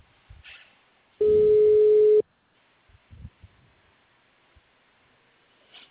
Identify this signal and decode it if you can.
Ringback tone (Europe, 425 Hz).